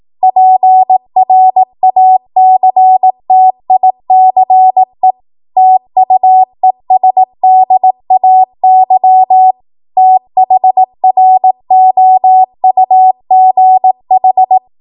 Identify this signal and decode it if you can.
Morse code, heard as a keyed tone, about 18 wpm, 750 Hz. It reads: PRACTICE TUESDAY THROUGH
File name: Archive / 18 WPM / 18WPM